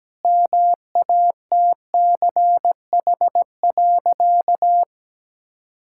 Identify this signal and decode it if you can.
Morse code, heard as a keyed tone, about 17 wpm, 700 Hz. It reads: MATCH.
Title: Sentences from Top 1600 Words 17wpm